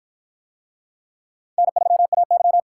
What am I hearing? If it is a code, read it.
Morse code, heard as a keyed tone, about 49 wpm, 700 Hz. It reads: N3AY